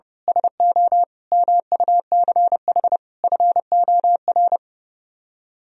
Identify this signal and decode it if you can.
Morse code, heard as a keyed tone, about 28 wpm, 700 Hz. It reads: SO MUCH FOR
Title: Speed-Racing - Sets of 5 Words 30wpm